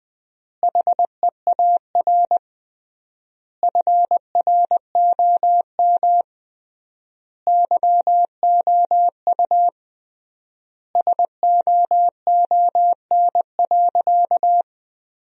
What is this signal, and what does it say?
Morse code, heard as a keyed tone, about 20 wpm, 700 Hz. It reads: HEAR FROM YOU SOON.
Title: 3x Word-Spacing - Sentences from Top 400 Words - 20wpm